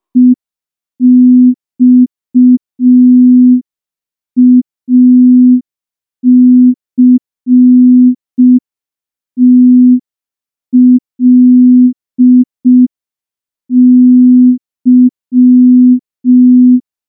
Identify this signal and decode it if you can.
Morse code, heard as a keyed tone, about 5 wpm, 250 Hz. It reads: EXACTLY
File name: morse / morse-training-05WPM